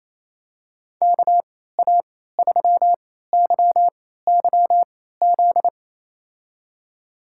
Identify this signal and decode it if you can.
Morse code, heard as a keyed tone, about 28 wpm, spaced as for 16 wpm, 700 Hz. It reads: KA3YYZ
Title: USA Call Signs - 6 Characters - 3x Character Spacing - 28wpm